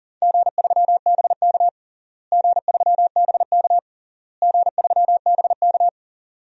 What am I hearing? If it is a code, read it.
Morse code, heard as a keyed tone, about 40 wpm, 700 Hz. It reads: G3BK G3BK G3BK
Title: DXCC-100 Call Signs - 4 Characters - Repeated 3x - 40wpm